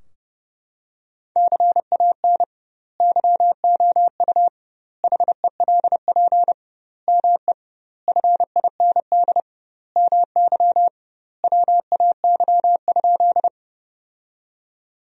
Morse code, heard as a keyed tone, about 32 wpm, 700 Hz. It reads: CAN YOU HELP ME FIND MY WAY?
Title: 2x Word-Spacing - Sentences from Top 200 Words - 30wpm